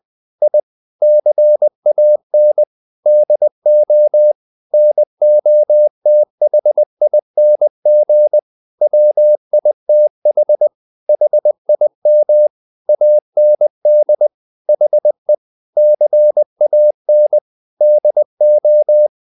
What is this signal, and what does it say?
Morse code, heard as a keyed tone, about 20 wpm, 600 Hz. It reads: I CAN DO NOTHING WITH HIM AND HE CAN DO
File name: the old man and the sea / the old man and the sea part 1 20wpm